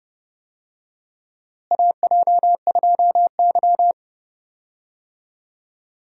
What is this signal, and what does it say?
Morse code, heard as a keyed tone, about 32 wpm, 700 Hz. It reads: AJ2Y
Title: USA Call Signs - 4 Characters - Repeated 3x - 30wpm